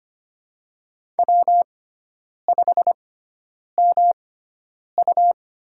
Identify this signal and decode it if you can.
Morse code, heard as a keyed tone, about 25 wpm, spaced as for 9 wpm, 700 Hz. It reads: W5MU
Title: USA Call Signs - 4 Characters - 6x Character Spacing - 25wpm